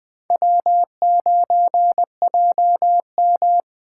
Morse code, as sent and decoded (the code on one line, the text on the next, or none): .-- ----. .--- --
W9JM